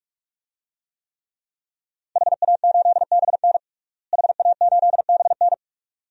Morse code, heard as a keyed tone, about 46 wpm, 700 Hz. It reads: HA8BN HA8BN
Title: DXCC-100 Call Signs - 5 Characters - Repeated 2x - 45wpm